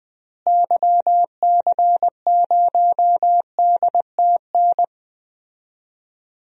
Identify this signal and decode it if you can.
Morse code, heard as a keyed tone, about 21 wpm, 700 Hz. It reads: YC0DTN
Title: DXCC-100 Call Signs - 6 Characters - Repeated 2x - 20wpm